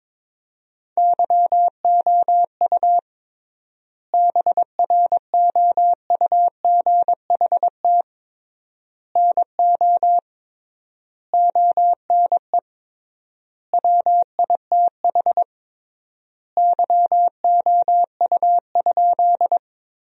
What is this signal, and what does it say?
Morse code, heard as a keyed tone, about 22 wpm, 700 Hz. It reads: YOU BROUGHT NO ONE WITH YOU?
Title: 3x Word-Spacing - Sentences from Top 500 Words - 22wpm